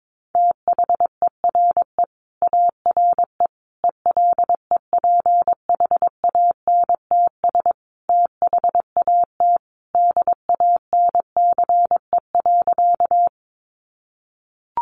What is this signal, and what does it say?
Morse code, heard as a keyed tone, about 22 wpm, 700 Hz. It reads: THERE ARE ELEPHANTS THAT DANCE.
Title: Easy Phrases 22wpm